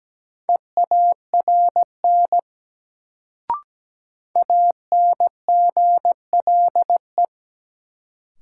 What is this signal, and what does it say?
Morse code, heard as a keyed tone, about 17 wpm, 700 Hz. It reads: EARN ANGLE
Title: Sentences from Top 1500 Words - New Word Drill - 17wpm